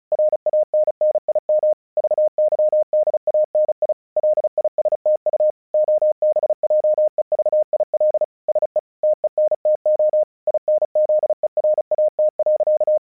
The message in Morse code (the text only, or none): RANNIM VYDANI LISTU OBJEVIL SE TENTO INZERAT.